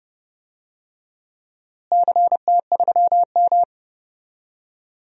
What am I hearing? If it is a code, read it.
Morse code, heard as a keyed tone, about 32 wpm, 700 Hz. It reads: CT3M
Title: DXCC-100 Call Signs - 4 Characters - Repeated 2x - 30wpm